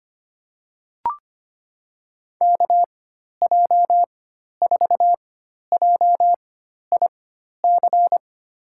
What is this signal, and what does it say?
Morse code, heard as a keyed tone, about 25 wpm, spaced as for 12 wpm, 700 Hz. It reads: KJ4JIC
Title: USA Call Signs - 4x Character Spacing - 25wpm